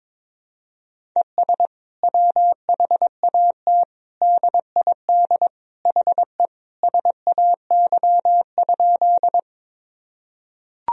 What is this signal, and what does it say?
Morse code, heard as a keyed tone, about 22 wpm, 700 Hz. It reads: ES WHAT DID HE SAY?
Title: Sentences from Top 200 Words and Abbreviations 22wpm